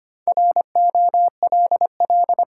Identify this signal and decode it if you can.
Morse code, heard as a keyed tone, about 25 wpm, 700 Hz. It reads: ROLL
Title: Four Letter Words - 25wpm